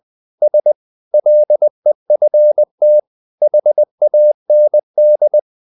Morse code, heard as a keyed tone, about 20 wpm, 600 Hz. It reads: S LEFT HAND
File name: the old man and the sea / the old man and the sea part 1 20wpm